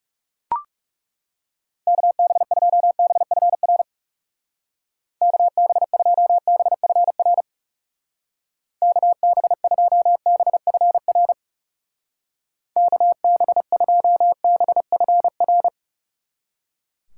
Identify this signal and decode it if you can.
Morse code, heard as a keyed tone, about 38 wpm, 700 Hz. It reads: KB2BFR KB2BFR KB2BFR KB2BFR